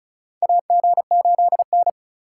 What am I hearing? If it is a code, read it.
Morse code, heard as a keyed tone, about 36 wpm, 700 Hz. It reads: AG8N
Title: USA Call Signs - 4 Characters - 4x Character Spacing - 35wpm